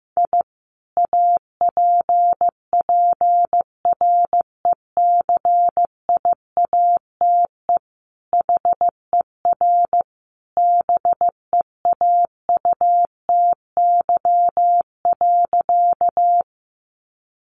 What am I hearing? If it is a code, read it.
Morse code, heard as a keyed tone, about 15 wpm, 700 Hz. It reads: I APPRECIATE HER BEAUTY.
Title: Sentences from Top 1300 Words 15wpm